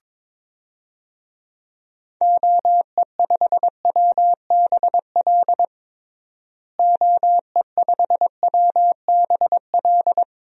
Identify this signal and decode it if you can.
Morse code, heard as a keyed tone, about 22 wpm, 700 Hz. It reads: OE5WBL OE5WBL
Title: DXCC-100 Call Signs - 6 Characters - Repeated 2x - 22wpm